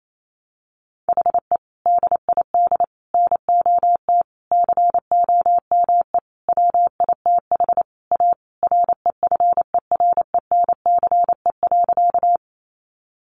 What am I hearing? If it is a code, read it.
Morse code, heard as a keyed tone, about 28 wpm, 700 Hz. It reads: HE DID NOT COME WITH A REFERENCE.
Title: Sentences from Top 1200 Words 28wpm